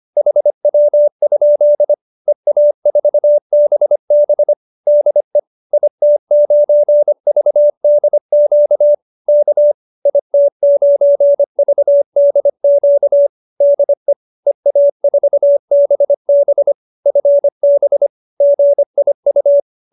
Morse code, heard as a keyed tone, about 25 wpm, 590 Hz. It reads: HW? EA4BB DE IT9VDQ K IT9VDQ DE EA4BB FB GIU